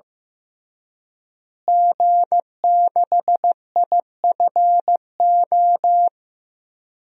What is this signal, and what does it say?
Morse code, heard as a keyed tone, about 15 wpm, 700 Hz. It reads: G6IFO
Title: DXCC-100 Call Signs - 5 Characters - Repeated 3x - 15wpm